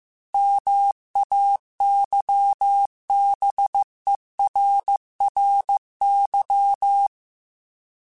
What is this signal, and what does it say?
Morse code, heard as a keyed tone, about 15 wpm, 800 Hz. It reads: MAYBERRY